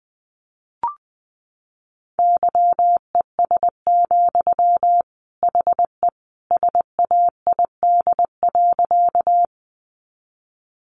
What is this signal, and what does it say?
Morse code, heard as a keyed tone, about 20 wpm, 700 Hz. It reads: YES, HE SAID.
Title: Sentences from Top 500 Words 20wpm